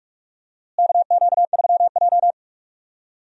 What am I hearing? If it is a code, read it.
Morse code, heard as a keyed tone, about 45 wpm, 700 Hz. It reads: KQ3J
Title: USA Call Signs - 4 Characters - Repeated 3x - 45wpm